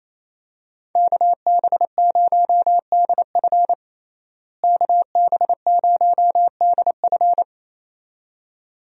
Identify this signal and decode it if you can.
Morse code, heard as a keyed tone, about 28 wpm, 700 Hz. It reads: KB0DF KB0DF
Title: USA Call Signs - 5 Characters - Repeated 2x - 28wpm